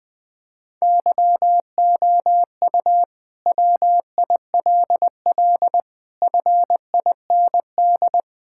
Morse code, heard as a keyed tone, about 20 wpm, 700 Hz. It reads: YOU WILL FIND
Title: Sentences from Top 1000 Words - Review Part 5 - 20wpm